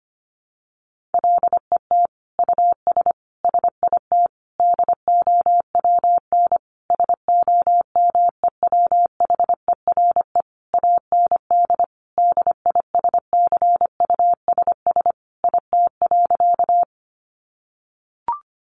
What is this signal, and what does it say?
Morse code, heard as a keyed tone, about 25 wpm, 700 Hz. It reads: LET US SIT DOWN SOMEWHERE AND DISCUSS IT.
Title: Sentences from Top 1400 Words 25wpm